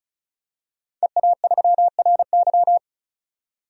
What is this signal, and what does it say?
Morse code, heard as a keyed tone, about 34 wpm, 700 Hz. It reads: EA3RY
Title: DXCC-100 Call Signs - 5 Characters - 35wpm